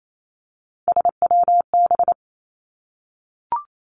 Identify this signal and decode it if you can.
Morse code, heard as a keyed tone, about 28 wpm, 700 Hz. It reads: SWB
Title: Three Challenging Letter-Number Combinations 28wpm